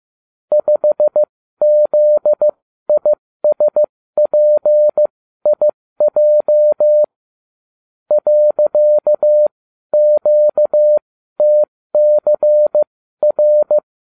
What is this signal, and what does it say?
Morse code, heard as a keyed tone, about 15 wpm, spaced as for 12 wpm, 600 Hz. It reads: 5ZISPIJ .QTCR